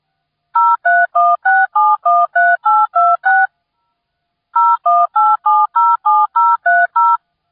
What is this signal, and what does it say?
DTMF, heard as telephone touch tones, tones of about 220 ms. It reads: 0316713826 018707030